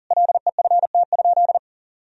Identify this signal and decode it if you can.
Morse code, heard as a keyed tone, about 40 wpm, 700 Hz. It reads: LEFT?
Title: Sentences from Top 1000 Words - Review Part 2 - 40wpm